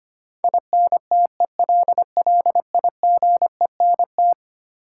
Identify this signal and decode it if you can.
Morse code, heard as a keyed tone, about 25 wpm, 700 Hz. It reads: INTELLIGENT
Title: Sentences from Top 1000 Words - Review Part 7 - New Word Drill - 25wpm